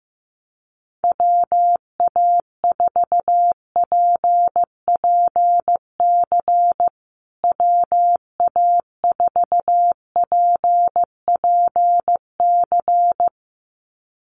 Morse code, heard as a keyed tone, about 15 wpm, 700 Hz. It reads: WA4PPC WA4PPC
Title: USA Call Signs Repeated 2x 15wpm